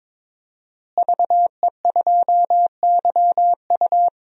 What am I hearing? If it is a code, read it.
Morse code, heard as a keyed tone, about 22 wpm, 700 Hz. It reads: VE2YU
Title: DXCC-100 Call Signs - 5 Characters - Repeated 2x - 22wpm